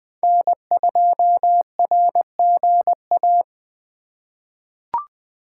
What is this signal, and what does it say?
Morse code, heard as a keyed tone, about 20 wpm, 700 Hz. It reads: N2RGA
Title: USA Call Signs - 5 Characters - 3x Character Spacing - 20wpm